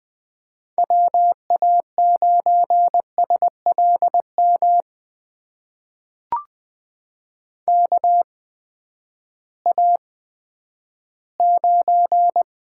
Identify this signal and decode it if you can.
Morse code, heard as a keyed tone, about 20 wpm, spaced as for 11 wpm, 700 Hz. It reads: WA9SLM KA9
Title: USA Call Signs - 6 Characters - 8x Character Spacing - 20wpm